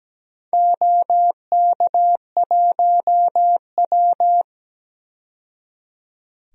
Morse code, heard as a keyed tone, about 17 wpm, 700 Hz. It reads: OK1W
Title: DXCC-100 Call Signs - 4 Characters - 17wpm